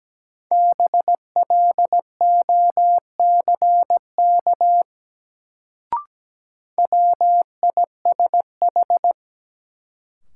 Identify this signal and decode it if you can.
Morse code, heard as a keyed tone, about 17 wpm, 700 Hz. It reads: BLOCK WISH
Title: Sentences from Top 1600 Words - New Word Drill Alt - 17wpm